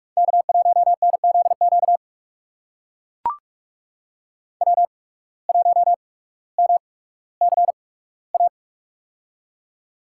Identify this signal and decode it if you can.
Morse code, heard as a keyed tone, about 42 wpm, spaced as for 19 wpm, 700 Hz. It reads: K1NZQ W1MCA
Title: USA Call Signs - 5 Characters - 8x Character Spacing - 45wpm